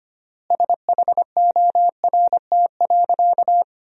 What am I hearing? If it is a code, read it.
Morse code, heard as a keyed tone, about 25 wpm, 700 Hz. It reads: SHORT.
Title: 3x Word-Spacing - Sentences from Top 500 Words - 25wpm